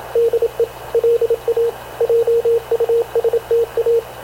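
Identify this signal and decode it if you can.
Morse code, heard as a keyed tone, about 28 wpm, 460 Hz, with radio noise. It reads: DE LA JUSTA
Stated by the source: Havana Radio / CLA sending Spanish press (PX) in Morse, recorded by Radio Officer Paul Dumesnil onboard CNAV QUEST/CZDO.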